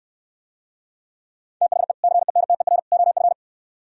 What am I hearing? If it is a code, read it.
Morse code, heard as a keyed tone, about 67 wpm, 700 Hz. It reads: THE CENTER OF